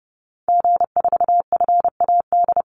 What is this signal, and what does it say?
Morse code, heard as a keyed tone, about 30 wpm, 700 Hz. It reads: G4FAD
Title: Call Signs Repeated 3x - 30wpm